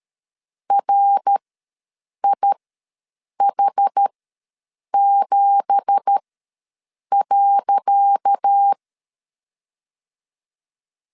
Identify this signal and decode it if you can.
Morse code, heard as a keyed tone, about 13 wpm, spaced as for 7 wpm, 800 Hz. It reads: RIH7.
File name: LEZIONI / 10 SIGNS / 10C DOT ws 12wpm 3548